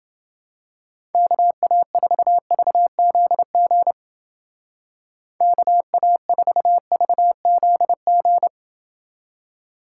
Morse code, heard as a keyed tone, about 29 wpm, 700 Hz. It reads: KA4VZG KA4VZG